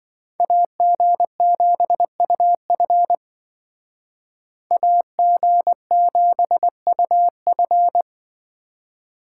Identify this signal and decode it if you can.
Morse code, heard as a keyed tone, about 27 wpm, 700 Hz. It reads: AG7UF AG7UF